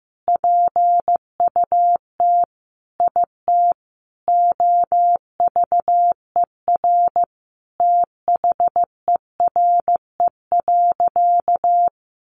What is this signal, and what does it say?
Morse code, heard as a keyed tone, about 15 wpm, 700 Hz. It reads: PUT IT OVER THERE.
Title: Sentences from Top 1100 Words 15wpm